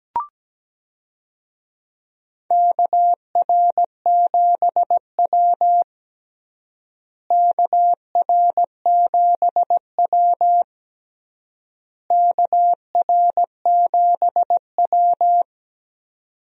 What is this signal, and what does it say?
Morse code, heard as a keyed tone, about 17 wpm, 700 Hz. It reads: KR7W KR7W KR7W
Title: USA Call Signs - 4 Characters - Repeated 3x - 17wpm